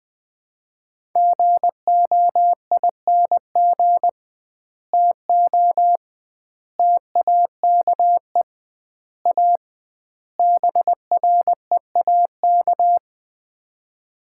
Morse code, heard as a keyed tone, about 20 wpm, 700 Hz. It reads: GOING TO TAKE A BREAK
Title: 2x Word-Spacing - Sets of 5 Words - 20wpm